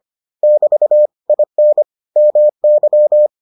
Morse code, heard as a keyed tone, about 25 wpm, 600 Hz. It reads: <BT> IN MY